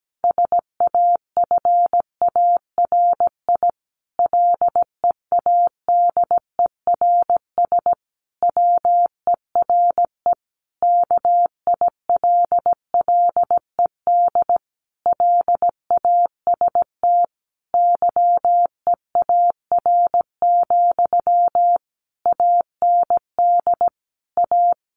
Morse code, heard as a keyed tone, about 17 wpm, 700 Hz. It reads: SAFARI LEADERS WERE KILLED LAST YEAR, AND A
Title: A Sound Of Thunder 17wpm